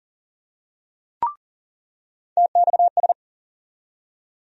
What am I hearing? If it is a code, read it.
Morse code, heard as a keyed tone, about 40 wpm, 700 Hz. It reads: TXS
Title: USA Call Sign Postfixes - 3 Letters 40wpm